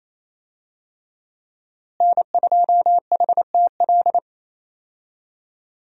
Morse code, heard as a keyed tone, about 28 wpm, 700 Hz. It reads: N2HTL
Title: USA Call Signs - 5 Characters - Repeated 3x - 28wpm